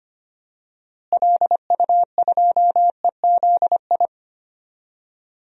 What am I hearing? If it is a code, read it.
Morse code, heard as a keyed tone, about 25 wpm, 700 Hz. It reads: LU2EZI
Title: DXCC-100 Call Signs - 6 Characters - Repeated 2x - 25wpm